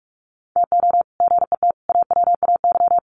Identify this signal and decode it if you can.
Morse code, heard as a keyed tone, about 44 wpm, 700 Hz. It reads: TO GET AWAY
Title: Sets of 5 Words Encore2 45wpm